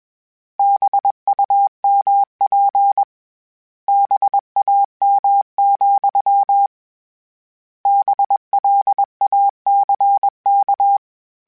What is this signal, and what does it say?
Morse code, heard as a keyed tone, about 21 wpm, 800 Hz. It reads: BUMP BAM, BLACK